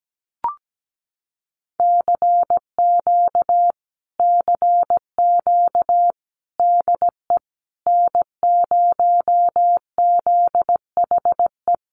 Morse code, heard as a keyed tone, about 17 wpm, 700 Hz. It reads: CQ CQ DE N0ZHE
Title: Common QSO Phrases Encore 17wpm